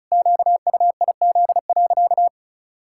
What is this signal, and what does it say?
Morse code, heard as a keyed tone, about 36 wpm, 700 Hz. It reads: QUIZ.
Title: Pangram Challenge v2 - 35wpm